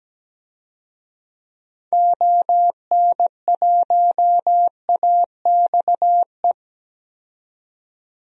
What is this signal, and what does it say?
Morse code, heard as a keyed tone, about 17 wpm, 700 Hz. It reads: ON1AXE
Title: DXCC-100 Call Signs - 6 Characters - 17wpm